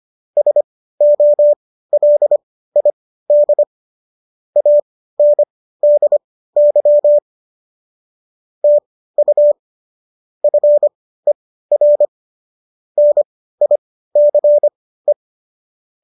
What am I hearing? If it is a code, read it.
Morse code, heard as a keyed tone, about 25 wpm, spaced as for 15 wpm, 590 Hz. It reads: SOLID ANDY TU FER NICE